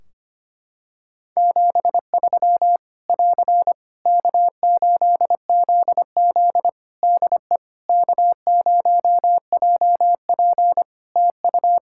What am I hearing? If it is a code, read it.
Morse code, heard as a keyed tone, about 25 wpm, 700 Hz. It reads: 73 <AR> K8ZZ DE K0JP TU
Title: Practice QSO - Exchange 4 - Ending - 25wpm